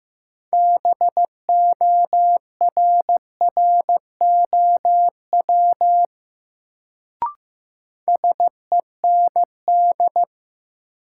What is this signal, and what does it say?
Morse code, heard as a keyed tone, about 15 wpm, 700 Hz. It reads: BORROW SEND
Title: Sentences from Top 1500 Words - New Word Drill - 15wpm